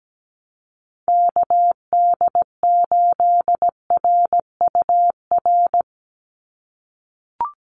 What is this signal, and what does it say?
Morse code, heard as a keyed tone, about 17 wpm, 700 Hz. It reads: KD8RUR
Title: USA Call Signs 17wpm